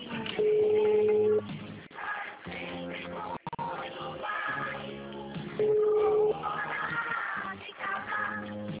Ringback tone (Europe, 425 Hz), with noise.